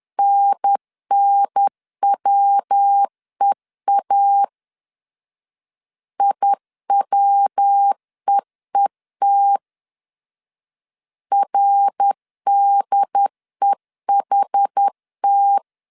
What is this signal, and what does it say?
Morse code, heard as a keyed tone, about 10 wpm, 800 Hz. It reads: NNWEA IWEET RDEHT
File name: CW / DET / 1 3 1500C S B 10wpm 3335